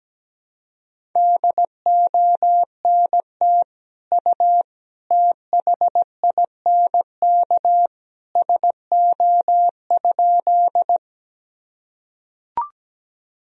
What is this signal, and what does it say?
Morse code, heard as a keyed tone, about 17 wpm, 700 Hz. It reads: DONT U THINK SO?
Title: Sentences from Top 300 Words and Abbreviations 17wpm